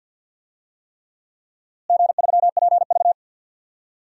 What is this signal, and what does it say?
Morse code, heard as a keyed tone, about 53 wpm, 700 Hz. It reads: G3PV
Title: DXCC-100 Call Signs - 4 Characters - Repeated 3x - 50wpm